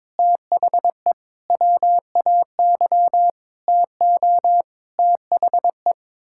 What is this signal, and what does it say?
Morse code, heard as a keyed tone, about 22 wpm, 700 Hz. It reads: THE WAY TO THE
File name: Sets of 9 Words - 22wpm